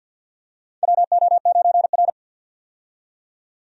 Morse code, heard as a keyed tone, about 52 wpm, 700 Hz. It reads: WO9R
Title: USA Call Signs - 4 Characters - 50wpm